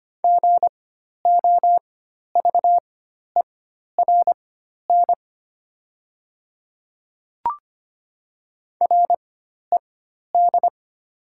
Morse code, heard as a keyed tone, about 25 wpm, spaced as for 12 wpm, 690 Hz. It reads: GOVERN RED
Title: 4x Character Spacing - Top 500 Words - 25wpm